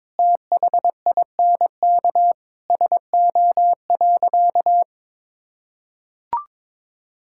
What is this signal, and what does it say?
Morse code, heard as a keyed tone, about 22 wpm, 700 Hz. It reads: THINK SO.
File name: practice / 22wpm / Sentences with 4-Word Phrases - 22wpm